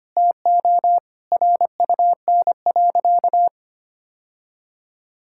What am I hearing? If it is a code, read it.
Morse code, heard as a keyed tone, about 25 wpm, 700 Hz. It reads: TO RUN.
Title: Sentences from Top 1000 Words - Review Part 5 - 25wpm